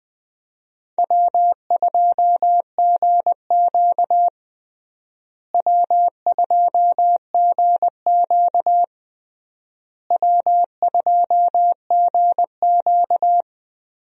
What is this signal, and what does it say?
Morse code, heard as a keyed tone, about 20 wpm, 700 Hz. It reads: W2GQ W2GQ W2GQ
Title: USA Call Signs - 4 Characters - Repeated 3x - 20wpm